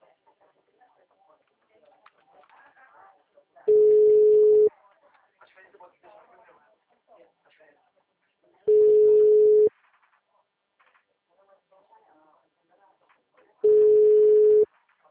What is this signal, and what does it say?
Ringback tone (Europe, 425 Hz), with speech.